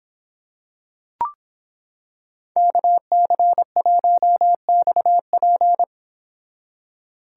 Morse code, heard as a keyed tone, about 26 wpm, 700 Hz. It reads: KC1XP